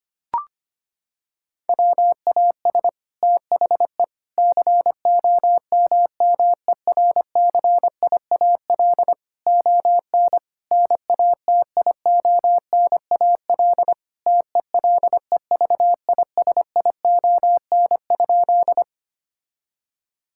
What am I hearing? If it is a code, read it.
Morse code, heard as a keyed tone, about 25 wpm, 700 Hz. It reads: WAS THE COMMERCIAL ON NATIONAL TELEVISION?
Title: Sentences from Top 1000 Words - Review Part 5 - 25wpm